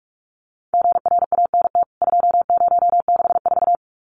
Morse code, heard as a keyed tone, about 44 wpm, 700 Hz. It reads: GRANT 2064
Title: CWT Exchange 45wpm